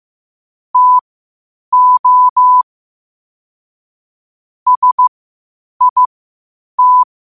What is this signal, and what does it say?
Morse code, heard as a keyed tone, about 15 wpm, spaced as for 8 wpm, 1000 Hz. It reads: TO SIT
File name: qotdfiles / 8wpm